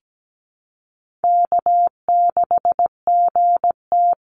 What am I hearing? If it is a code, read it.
Morse code, heard as a keyed tone, about 17 wpm, 700 Hz. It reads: K6GT
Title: Call Signs Repeated 2x - 17wpm